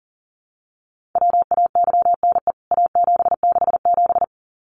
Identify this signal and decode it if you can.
Morse code, heard as a keyed tone, about 40 wpm, 700 Hz. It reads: WAYNE A767
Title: CWT Exchange 40wpm